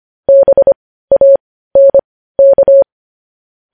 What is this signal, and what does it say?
Morse code, heard as a keyed tone, about 25 wpm, spaced as for 15 wpm, 550 Hz. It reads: BANK